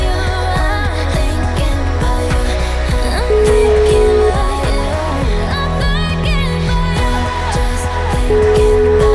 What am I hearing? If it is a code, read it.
Ringback tone (Europe, 425 Hz), with noise.